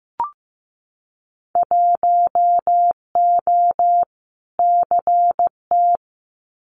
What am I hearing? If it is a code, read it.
Morse code, heard as a keyed tone, about 15 wpm, 700 Hz. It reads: 1O CT
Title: Winter Field Day Exchanges 15wpm